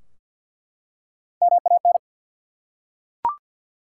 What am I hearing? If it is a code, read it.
Morse code, heard as a keyed tone, about 47 wpm, 700 Hz. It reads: MAN
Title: Three Letter Words - 50wpm